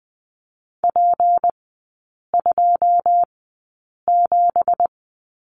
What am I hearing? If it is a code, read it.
Morse code, heard as a keyed tone, about 20 wpm, spaced as for 8 wpm, 690 Hz. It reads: P27
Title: 172 Lesson 28e - Number 9 - Three Characters 20wpm